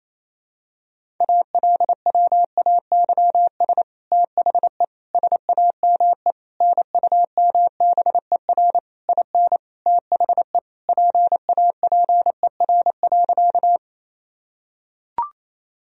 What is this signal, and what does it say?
Morse code, heard as a keyed tone, about 28 wpm, 700 Hz. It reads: ALWAYS THE SAME NUMBER IN THE PAPER.
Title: Sentences from Top 1000 Words - Review Part 1 - 28wpm